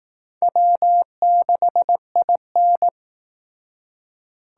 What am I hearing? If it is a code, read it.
Morse code, heard as a keyed tone, about 19 wpm, 700 Hz. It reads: W6IN